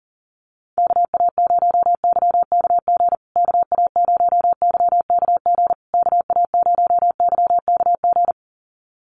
Morse code, heard as a keyed tone, about 40 wpm, 700 Hz. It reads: KA0YKG KA0YKG KA0YKG
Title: USA Call Signs Repeated 3x 40wpm